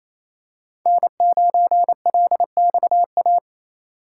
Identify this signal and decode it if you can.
Morse code, heard as a keyed tone, about 28 wpm, 700 Hz. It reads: N9LXA
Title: USA Call Signs - 5 Characters - 2x Character Spacing - 28wpm